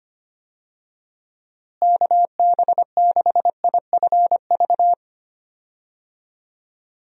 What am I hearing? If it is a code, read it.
Morse code, heard as a keyed tone, about 25 wpm, 700 Hz. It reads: KB6IFV
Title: USA Call Signs - 6 Characters - Repeated 3x - 25wpm